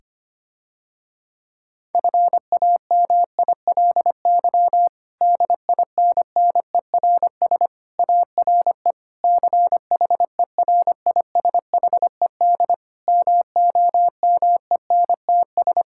Morse code, heard as a keyed tone, about 25 wpm, 700 Hz. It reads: FAMILY DINNERS ARE CHERISHED MOMENTS
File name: practice / 25wpm / Difficult Sentences with Hints - 25wpm